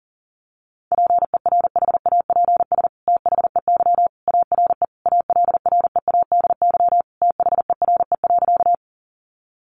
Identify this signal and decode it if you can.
Morse code, heard as a keyed tone, about 40 wpm, 700 Hz. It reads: PERHAPS THEY ARE ALREADY THERE.
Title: Sentences from Top 700 Words 40wpm